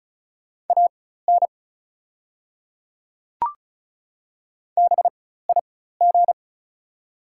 Morse code, heard as a keyed tone, about 36 wpm, spaced as for 16 wpm, 690 Hz. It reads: AN BIG
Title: 4x Character Spacing - Top 500 Words - 35wpm